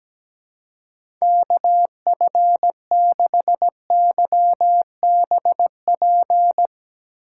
Morse code, heard as a keyed tone, about 17 wpm, 700 Hz. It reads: KF6YBP
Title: USA Call Signs - 6 Characters - 3x Character Spacing - 17wpm